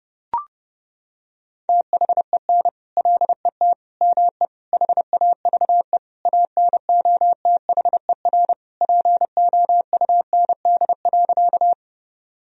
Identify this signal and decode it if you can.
Morse code, heard as a keyed tone, about 28 wpm, 700 Hz. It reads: THEN LET ME HAVE ANOTHER POUND.
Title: Sentences from Top 1000 Words - Review Part 2 - 30wpm